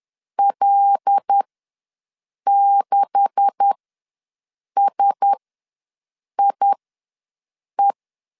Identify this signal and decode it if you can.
Morse code, heard as a keyed tone, about 10 wpm, spaced as for 6 wpm, 800 Hz. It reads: L6SIE
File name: LEZIONI / 10 SIGNS / 10B INT ws 10wpm 4134